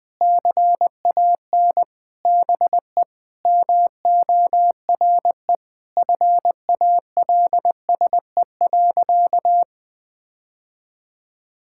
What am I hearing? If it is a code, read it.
Morse code, heard as a keyed tone, about 20 wpm, 700 Hz. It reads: CAN BE MORE FALSE.
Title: Sentences from Top 1600 Words 20wpm